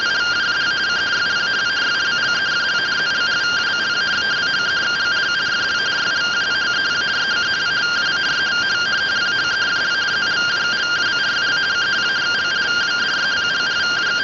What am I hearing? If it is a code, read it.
Two-tone teleprinter signal, about 170 Hz shift.
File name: findings / MR AXIAL 0mm 6mm.gen-08 rtty rx